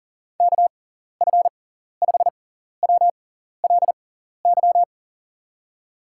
Morse code, heard as a keyed tone, about 42 wpm, spaced as for 14 wpm, 700 Hz. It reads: KF5WLY
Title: USA Call Signs - 6x Character Spacing - 40wpm